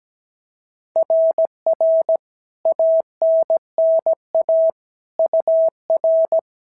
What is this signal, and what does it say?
Morse code, heard as a keyed tone, about 16 wpm, 650 Hz. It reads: RR ANNA UR
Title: SOTA QSO Practice 17wpm